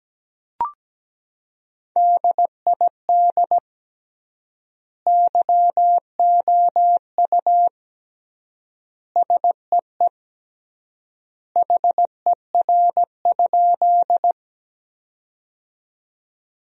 Morse code, heard as a keyed tone, about 17 wpm, 700 Hz. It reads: DID YOU SEE HER?
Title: 3x Word-Spacing - Sentences from Top 100 Words - 17wpm